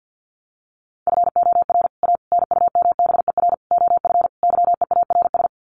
Morse code, heard as a keyed tone, about 50 wpm, 700 Hz. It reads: FOR A NUMBER OF YEARS
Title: Sets of 5 Words Encore 50wpm